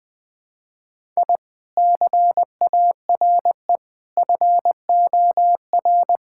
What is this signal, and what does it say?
Morse code, heard as a keyed tone, about 21 wpm, 700 Hz. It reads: I CARE FOR
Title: Sentences from Top 1000 Words - Review Part 6 - 20wpm